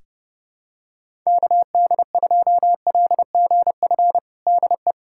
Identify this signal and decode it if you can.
Morse code, heard as a keyed tone, about 30 wpm, 700 Hz. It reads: KD2LGF DE
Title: USA Callsign DE Callsign - 6 Characters - 30wpm